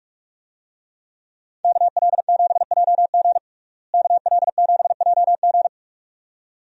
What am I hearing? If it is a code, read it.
Morse code, heard as a keyed tone, about 47 wpm, 700 Hz. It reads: KL7JG KL7JG
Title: DXCC-100 Call Signs - 5 Characters - Repeated 2x - 45wpm